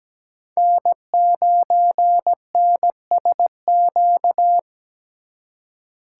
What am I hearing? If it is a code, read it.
Morse code, heard as a keyed tone, about 20 wpm, 700 Hz. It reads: N9NSQ
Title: Speed Racing - USA 1x3 Call Signs 17wpm